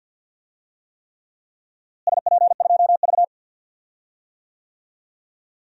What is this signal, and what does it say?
Morse code, heard as a keyed tone, about 47 wpm, 700 Hz. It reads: SP2V
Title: DXCC-100 Call Signs - 4 Characters - Repeated 3x - 50wpm